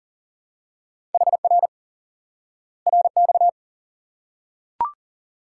Morse code, heard as a keyed tone, about 40 wpm, 700 Hz. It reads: HR RX
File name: IQE Course 3 - Lesson 89 - YAGI - Two Elements - 40wpm